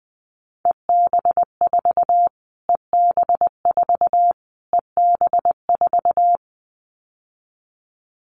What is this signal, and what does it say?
Morse code, heard as a keyed tone, about 20 wpm, 700 Hz. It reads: EB4 EB4 EB4
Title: Callsign Prefixes for Top 25 DXCC - Repeated 3x - 20wpm